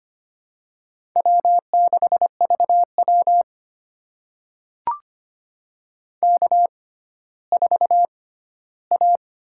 Morse code, heard as a keyed tone, about 25 wpm, spaced as for 14 wpm, 700 Hz. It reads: W6VW K4A
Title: USA Call Signs - 6x Character Spacing - 25wpm